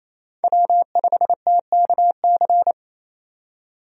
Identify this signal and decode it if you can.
Morse code, heard as a keyed tone, about 28 wpm, 700 Hz. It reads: W5TKC
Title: USA Call Signs - 5 Characters - 28wpm